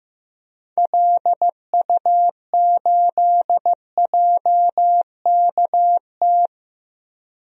Morse code, heard as a keyed tone, about 15 wpm, 700 Hz. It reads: LU8JKT
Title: DXCC-100 Call Signs - 6 Characters - Repeated 3x - 15wpm